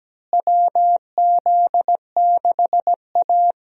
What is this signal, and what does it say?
Morse code, heard as a keyed tone, about 17 wpm, 700 Hz. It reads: WZ6A
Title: USA Call Signs - 4 Characters - Repeated 2x - 17wpm